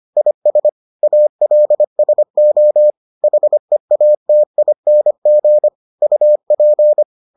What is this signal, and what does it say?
Morse code, heard as a keyed tone, about 25 wpm, 600 Hz. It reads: IS ALSO HEATING UP